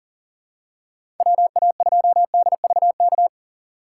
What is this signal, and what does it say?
Morse code, heard as a keyed tone, about 39 wpm, 700 Hz. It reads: WA2DVK
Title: USA Call Signs - 6 Characters - 4x Character Spacing - 40wpm